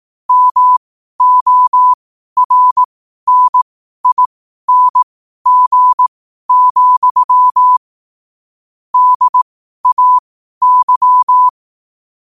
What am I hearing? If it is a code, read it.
Morse code, heard as a keyed tone, about 18 wpm, spaced as for 12 wpm, 1000 Hz. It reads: MORNING, DAY
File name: qotdfiles / 12wpm